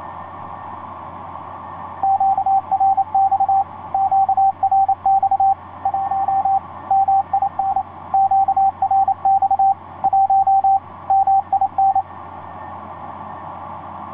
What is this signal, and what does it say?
Morse code, heard as a keyed tone, about 28 wpm, 780 Hz, with radio noise. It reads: QRX QRX 1 MIN QRX 1 MIN
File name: 6w-gm4fdm f6bee etc 15m cw 30jan2010 1502